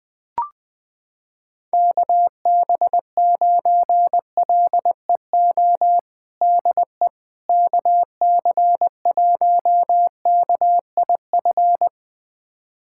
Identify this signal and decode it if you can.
Morse code, heard as a keyed tone, about 20 wpm, 700 Hz. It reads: KB9LEO DE KC1KIF
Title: USA Callsign DE Callsign - 6 Characters - 20wpm